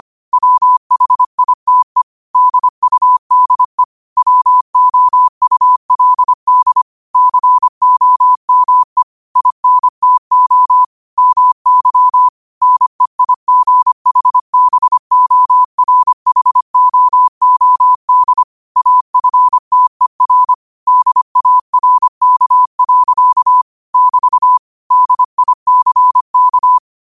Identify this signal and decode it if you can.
Morse code, heard as a keyed tone, about 25 wpm, 1000 Hz. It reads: WHITE DUDE WOULD COME INTO MY NEIGHBORHOOD AFTER DARK. <BT> DICK
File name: 25wpm